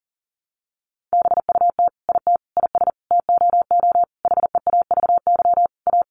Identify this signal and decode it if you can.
Morse code, heard as a keyed tone, about 40 wpm, 700 Hz. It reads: BUT IT IS TOO HEAVY A
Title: Sentences from Top 1100 Words 40wpm